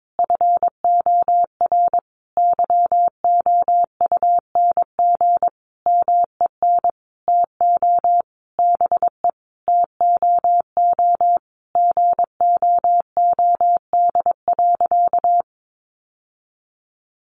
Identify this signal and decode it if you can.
Morse code, heard as a keyed tone, about 22 wpm, 700 Hz. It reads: FOR YOUNG MEN TO BE TOO GOOD.
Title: Sentences from Top 1200 Words 22wpm